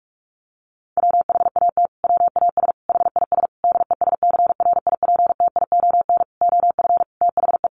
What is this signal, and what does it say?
Morse code, heard as a keyed tone, about 44 wpm, 700 Hz. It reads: WHAT WAS HIS DESCRIPTION OF THE
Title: Sentences from Top 1300 Words 45wpm